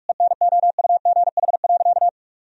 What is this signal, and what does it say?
Morse code, heard as a keyed tone, about 41 wpm, 700 Hz. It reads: ENOUGH.
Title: Sentences from Top 1000 Words - Review Part 3 - 45wpm